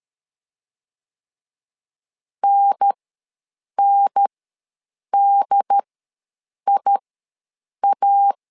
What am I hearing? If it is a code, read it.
Morse code, heard as a keyed tone, about 13 wpm, spaced as for 7 wpm, 800 Hz. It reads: NNDIA
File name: LEZIONI / 2 UHDN / 2 UHDN ws 12wpm 1708